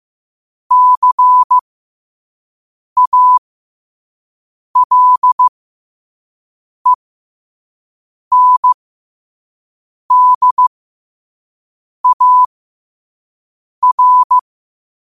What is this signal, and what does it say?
Morse code, heard as a keyed tone, about 15 wpm, spaced as for 5 wpm, 1000 Hz. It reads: CALENDAR
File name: qotdfiles / 5wpm